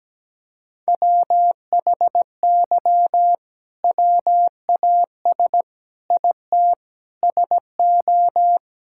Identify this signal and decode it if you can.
Morse code, heard as a keyed tone, about 17 wpm, 700 Hz. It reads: WHY WAS IT SO
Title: Sentences from Top 1000 Words - Review Part 5 - 17wpm